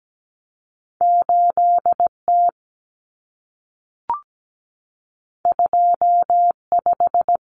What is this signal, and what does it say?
Morse code, heard as a keyed tone, about 17 wpm, 700 Hz. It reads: 8T 25
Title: Common Cut Numbers - 2-Digit 17wpm